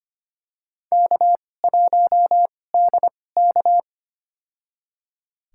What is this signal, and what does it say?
Morse code, heard as a keyed tone, about 25 wpm, spaced as for 18 wpm, 700 Hz. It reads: K1DK
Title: USA Call Signs - 4 Characters - 2x Character Spacing - 25wpm